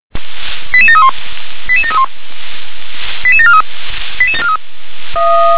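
Five-tone selective call (ZVEI), heard as quick runs of five tones, tones of about 70 ms. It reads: 88731 88731 88743
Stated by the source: ZVEI five-tone call received by SDR.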